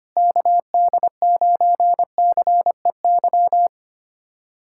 Morse code, heard as a keyed tone, about 25 wpm, 700 Hz. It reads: KD9CEY
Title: USA Callsign DE Callsign 25wpm